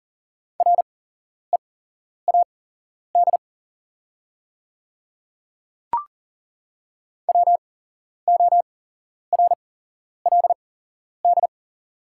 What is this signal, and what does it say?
Morse code, heard as a keyed tone, about 41 wpm, spaced as for 11 wpm, 690 Hz. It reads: READ WORLD
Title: 8x Character Spacing - Top 300 Words - 40wpm